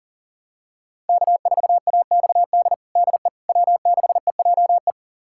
Morse code, heard as a keyed tone, about 37 wpm, 700 Hz. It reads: K4AXD DE W6EJE
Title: USA Callsign DE Callsign 40wpm